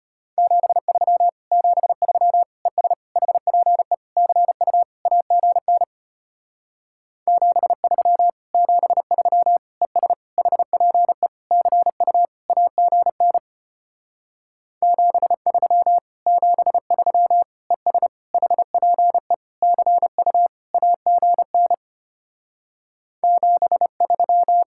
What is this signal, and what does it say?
Morse code, heard as a keyed tone, about 33 wpm, 700 Hz. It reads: 73 73 ES HPE CU AGN 73 73 ES HPE CU AGN 73 73 ES HPE CU AGN 73